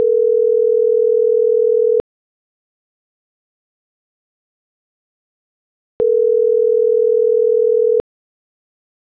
Ringback tone (North America), 440 + 480 Hz.